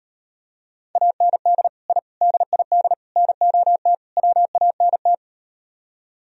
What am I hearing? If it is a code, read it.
Morse code, heard as a keyed tone, about 38 wpm, 700 Hz. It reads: AND I DID NOT WANT